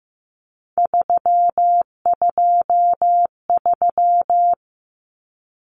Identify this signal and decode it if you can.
Morse code, heard as a keyed tone, about 15 wpm, 700 Hz. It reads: 323
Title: Common Cut Numbers - 3-Digit 15wpm